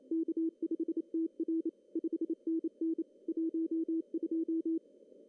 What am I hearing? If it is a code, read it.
Morse code, heard as a keyed tone, about 27 wpm, 330 Hz. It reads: K5TR 5NN 12
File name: CW / cqww-cw-k5tr